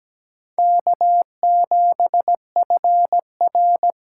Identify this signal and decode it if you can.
Morse code, heard as a keyed tone, about 20 wpm, 700 Hz. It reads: K7FR
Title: Speed Racing - USA 1x2 Call Signs 17wpm